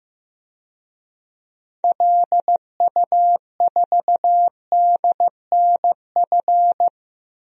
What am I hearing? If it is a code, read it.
Morse code, heard as a keyed tone, about 15 wpm, 700 Hz. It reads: LU4DNF
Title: DXCC-100 Call Signs - 6 Characters - 15wpm